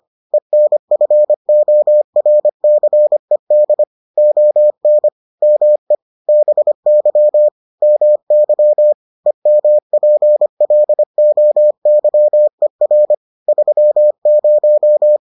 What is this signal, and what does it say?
Morse code, heard as a keyed tone, about 25 wpm, 600 Hz. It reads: ENFORCED ON ME BY MY EMPLOYER 30